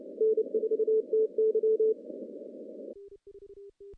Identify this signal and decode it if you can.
Morse code, heard as a keyed tone, about 29 wpm, 440 Hz. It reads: N4TY
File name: contests / CW-SS-k5tr- -0100